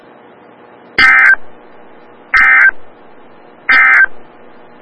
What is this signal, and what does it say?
Emergency Alert System (SAME) tones, heard as harsh buzzing bursts. It is heard from 0.9 to 4.1 s.
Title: Des Moines, IA NOAA Weather Radio - WXL57